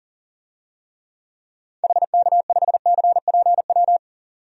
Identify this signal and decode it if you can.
Morse code, heard as a keyed tone, about 40 wpm, 700 Hz. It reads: HK5CPW
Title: DXCC-100 Call Signs - 6 Characters - 40wpm